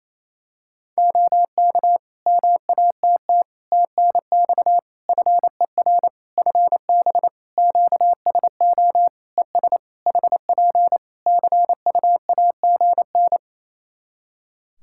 Morse code, heard as a keyed tone, about 28 wpm, 700 Hz. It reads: OK MATT TNX FER FB QSO ES HP CUAGN
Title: Practice QSO - Exchange 4 - Ending - 28wpm